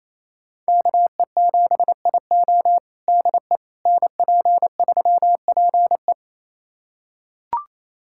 Morse code, heard as a keyed tone, about 28 wpm, 700 Hz. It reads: KE7IO DE NP3PE
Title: USA Callsign DE Callsign - 5 Characters - 28wpm